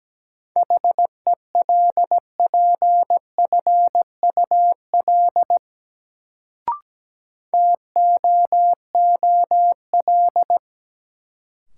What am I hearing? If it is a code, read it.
Morse code, heard as a keyed tone, about 17 wpm, 700 Hz. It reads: HELPFUL TOOL
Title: Sentences from Top 1000 Words - Review Part 6 - New Word Drill - 17wpm